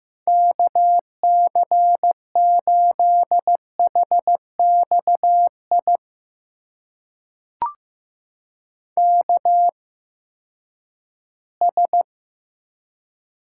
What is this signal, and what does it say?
Morse code, heard as a keyed tone, about 15 wpm, spaced as for 10 wpm, 700 Hz. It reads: KC8HXI KS
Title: USA Call Signs - 8x Character Spacing - 15wpm